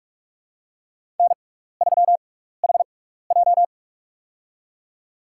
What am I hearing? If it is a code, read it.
Morse code, heard as a keyed tone, about 44 wpm, spaced as for 15 wpm, 700 Hz. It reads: N3HJ
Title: USA Call Signs - 4 Characters - 6x Character Spacing - 45wpm